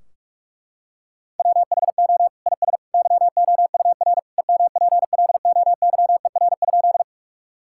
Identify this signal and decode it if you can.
Morse code, heard as a keyed tone, about 46 wpm, 700 Hz. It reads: WHO IS YOUR EMPLOYER?
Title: Sentences from Top 1600 Words 45wpm